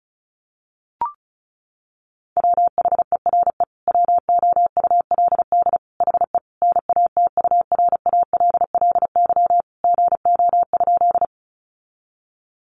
Morse code, heard as a keyed tone, about 34 wpm, 700 Hz. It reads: WHERE WOULD HE NATURALLY GO?
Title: Sentences from Top 1300 Words 35wpm